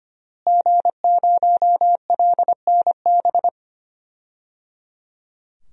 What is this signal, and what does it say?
Morse code, heard as a keyed tone, about 25 wpm, 700 Hz. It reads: G0LNB
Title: DXCC-100 Call Signs - 5 Characters - Repeated 3x - 25wpm